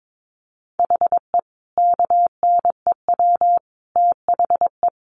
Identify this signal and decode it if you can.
Morse code, heard as a keyed tone, about 22 wpm, 700 Hz. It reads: HE KNEW THE
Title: Sentences from Top 1300 Words 22wpm